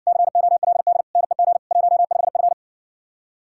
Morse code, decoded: COLD NEG 15F

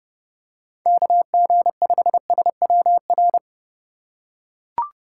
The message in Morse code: KG5SWR

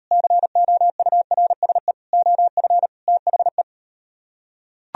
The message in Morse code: COURSE OF THE

